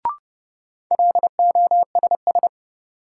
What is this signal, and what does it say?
Morse code, heard as a keyed tone, about 29 wpm, 700 Hz. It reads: LOSS
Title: Sentences from Top 1400 Words - New Word Drill Alt - 30wpm